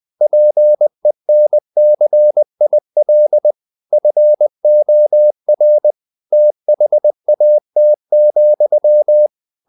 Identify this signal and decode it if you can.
Morse code, heard as a keyed tone, about 20 wpm, 600 Hz. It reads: PENCIL FOR THAT,
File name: the old man and the sea part 2 20wpm